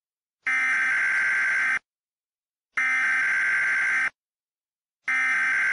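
Emergency Alert System (SAME) tones, heard as harsh buzzing bursts. It is heard from 0.8 to 5.0 s.
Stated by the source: Emergency broadcast and ham radio signals in a fictional audio drama after a world-ending eruption.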